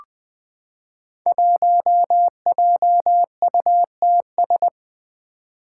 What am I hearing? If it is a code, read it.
Morse code, heard as a keyed tone, about 20 wpm, 700 Hz. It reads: 1JUTS
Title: Five Challenging Letter-Number Combinations 20wpm